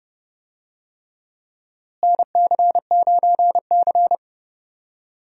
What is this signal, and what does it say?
Morse code, heard as a keyed tone, about 30 wpm, 700 Hz. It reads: NC9C
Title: USA Call Signs - 4 Characters - Repeated 3x - 30wpm